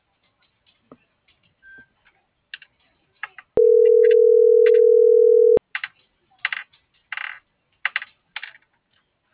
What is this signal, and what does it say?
Ringback tone (North America), 440 + 480 Hz.